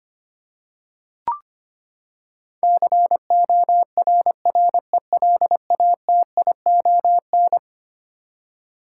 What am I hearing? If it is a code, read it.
Morse code, heard as a keyed tone, about 25 wpm, 700 Hz. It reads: CORRELATION
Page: morse-code-ninja-practice-25wpm